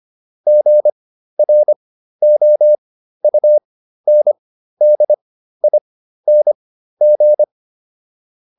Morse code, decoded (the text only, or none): GROUNDING